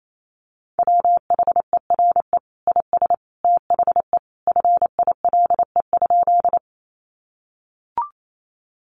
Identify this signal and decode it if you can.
Morse code, heard as a keyed tone, about 28 wpm, 700 Hz. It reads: WHERE IS THE FILE?
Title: Sentences from Top 1200 Words 28wpm